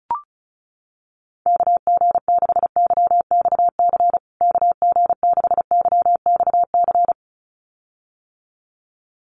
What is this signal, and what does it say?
Morse code, heard as a keyed tone, about 34 wpm, 700 Hz. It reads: KG6YXC KG6YXC
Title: USA Call Signs Repeated 2x 35wpm